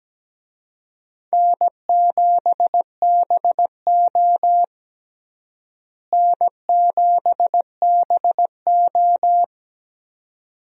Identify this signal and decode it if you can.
Morse code, heard as a keyed tone, about 17 wpm, 700 Hz. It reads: N7BO N7BO
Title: USA Call Signs - 4 Characters - Repeated 2x - 17wpm